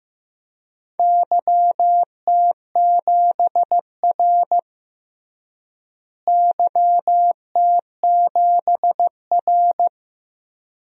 Morse code, heard as a keyed tone, about 15 wpm, 700 Hz. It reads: YT7R YT7R
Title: DXCC-100 Call Signs - 4 Characters - Repeated 2x - 15wpm